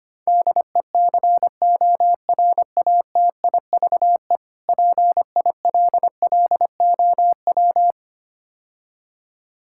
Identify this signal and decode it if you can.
Morse code, heard as a keyed tone, about 26 wpm, 700 Hz. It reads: DECORATIVE PILLOW
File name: practice / 25wpm / Difficult Words - Speed-Racing with Hints - 25wpm